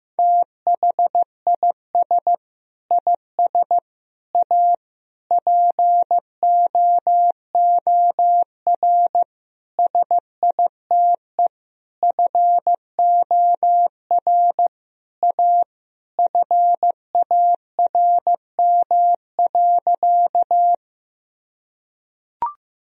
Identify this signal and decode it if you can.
Morse code, heard as a keyed tone, about 16 wpm, 700 Hz. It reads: THIS IS A POOR SITE FOR A FARM.
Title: Sentences from Top 1000 Words - Review Part 4 - 15wpm